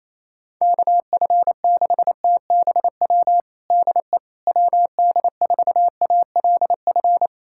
Morse code, heard as a keyed tone, about 28 wpm, 700 Hz. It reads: KF6TBW DE WD4ALF
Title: USA Callsign DE Callsign - 6 Characters - 28wpm